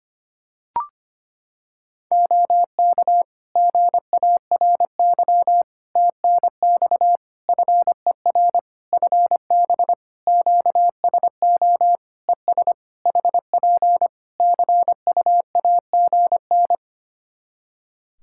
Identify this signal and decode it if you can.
Morse code, heard as a keyed tone, about 25 wpm, 700 Hz. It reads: OK GARY TNX FER FB QSO ES HP CUAGN
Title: Practice QSO - Exchange 4 - Ending - 25wpm